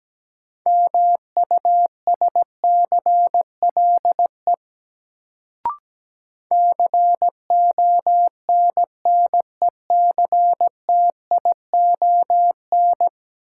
Morse code, heard as a keyed tone, about 17 wpm, 700 Hz. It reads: MUSCLE CONNECTION
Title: Sentences from Top 1300 Words - New Word Drill Alt - 17wpm